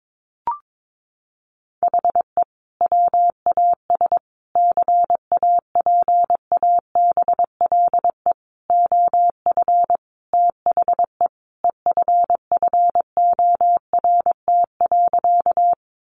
Morse code, heard as a keyed tone, about 22 wpm, 700 Hz. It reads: HE WAS CAPABLE OF THE EFFORT.
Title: Sentences from Top 1200 Words 22wpm